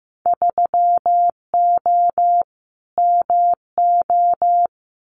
Morse code, heard as a keyed tone, about 15 wpm, 700 Hz. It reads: 3O MO
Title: Winter Field Day Exchanges Encore 15wpm